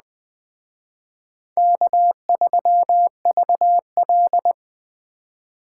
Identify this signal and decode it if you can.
Morse code, heard as a keyed tone, about 20 wpm, 700 Hz. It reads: K3VL